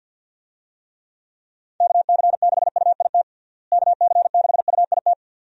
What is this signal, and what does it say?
Morse code, heard as a keyed tone, about 49 wpm, 700 Hz. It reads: KC6UIT KC6UIT
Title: USA Call Signs - 6 Characters - Repeated 2x - 50wpm